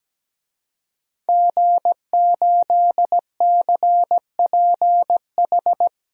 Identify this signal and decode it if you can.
Morse code, heard as a keyed tone, about 17 wpm, 700 Hz. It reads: G8CPH
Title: DXCC-100 Call Signs - 5 Characters - 17wpm